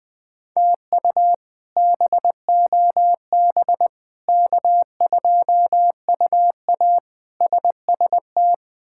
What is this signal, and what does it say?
Morse code, heard as a keyed tone, about 20 wpm, 700 Hz. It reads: TU BOB K2UA SST
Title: K1USN SST QSO Practice - 20wpm